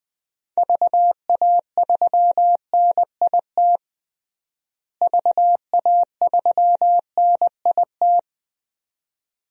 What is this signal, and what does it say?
Morse code, heard as a keyed tone, about 20 wpm, 700 Hz. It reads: VA3NIT VA3NIT
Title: DXCC-100 Call Signs - 6 Characters - Repeated 2x - 20wpm